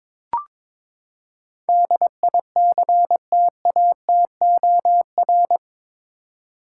Morse code, heard as a keyed tone, about 22 wpm, 700 Hz. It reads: DICTATOR
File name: practice / 22wpm / Eight Letter Words - 22wpm